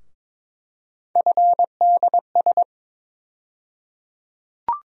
Morse code, heard as a keyed tone, about 21 wpm, 700 Hz. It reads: FDS
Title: USA Call Sign Postfixes - 3 Letters 22wpm